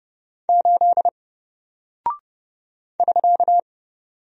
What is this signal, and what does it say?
Morse code, heard as a keyed tone, about 29 wpm, 700 Hz. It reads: 8 <SK>
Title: Single Character (Letter:Number:Punctuation:Prosign) 30wpm